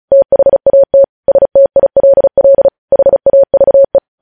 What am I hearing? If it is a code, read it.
Morse code, heard as a keyed tone, about 36 wpm, 550 Hz. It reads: THAT STILL HAVE
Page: news headlines in morse code at 35 wpm